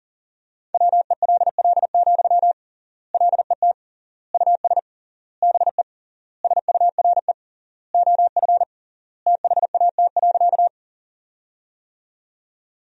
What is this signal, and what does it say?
Morse code, heard as a keyed tone, about 40 wpm, 700 Hz. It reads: WELL, LET US BE SURE OF THAT.
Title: 3x Word-Spacing - Sentences from Top 300 Words - 40wpm